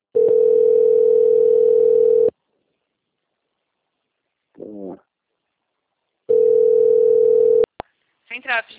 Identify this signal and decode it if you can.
Ringback tone (North America), 440 + 480 Hz, with speech.
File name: out- - - - - .43332